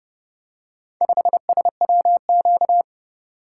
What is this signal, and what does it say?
Morse code, heard as a keyed tone, about 29 wpm, 700 Hz. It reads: 5SWQ